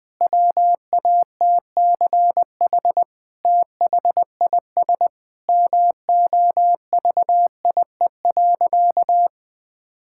Morse code, heard as a keyed tone, about 19 wpm, 700 Hz. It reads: WATCH THIS MOVIE.
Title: Sentences from Top 1000 Words - Review Part 5 - 20wpm